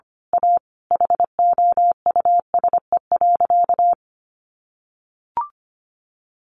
Morse code, heard as a keyed tone, about 25 wpm, 700 Hz. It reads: A HOUSE.
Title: Sentences from Top 600 Words 25wpm